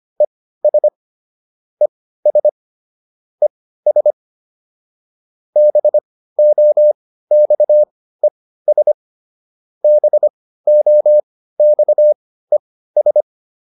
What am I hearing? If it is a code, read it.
Morse code, heard as a keyed tone, about 25 wpm, spaced as for 15 wpm, 600 Hz. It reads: ES ES ES BOXES BOXES